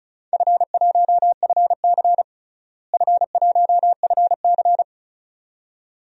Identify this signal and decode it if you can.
Morse code, heard as a keyed tone, about 34 wpm, 700 Hz. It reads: F1FC F1FC
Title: DXCC-100 Call Signs - 4 Characters - Repeated 2x - 35wpm